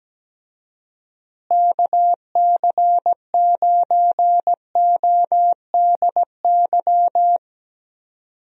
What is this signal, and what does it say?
Morse code, heard as a keyed tone, about 17 wpm, 700 Hz. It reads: KC9ODY
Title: USA Call Signs - 6 Characters - Repeated 3x - 17wpm